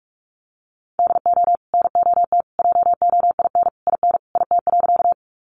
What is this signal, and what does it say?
Morse code, heard as a keyed tone, about 44 wpm, 700 Hz. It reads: DO NOT JOIN IN IT.
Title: Sentences from Top 1400 Words 45wpm